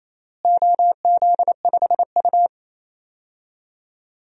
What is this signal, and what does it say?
Morse code, heard as a keyed tone, about 28 wpm, 700 Hz. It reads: OZ5U
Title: DXCC-100 Call Signs - 4 Characters - Repeated 3x - 28wpm